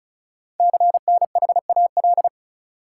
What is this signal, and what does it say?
Morse code, heard as a keyed tone, about 36 wpm, 690 Hz. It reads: CNHAL
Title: Code Groups - 5 Letters - 35wpm